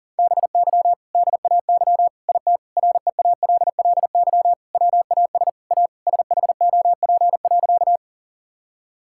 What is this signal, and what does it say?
Morse code, heard as a keyed tone, about 40 wpm, 700 Hz. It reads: BY DAY IT REALLY WAS A SHOP.
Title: Sentences from Top 1600 Words 40wpm